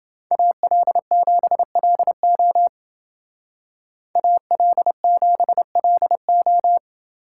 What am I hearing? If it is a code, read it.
Morse code, heard as a keyed tone, about 29 wpm, 700 Hz. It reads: AL7LO AL7LO